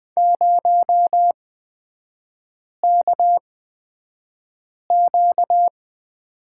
Morse code, heard as a keyed tone, about 20 wpm, spaced as for 5 wpm, 700 Hz. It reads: 0KQ
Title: 243 Lesson 39e - Letter z - Three Characters 20wpm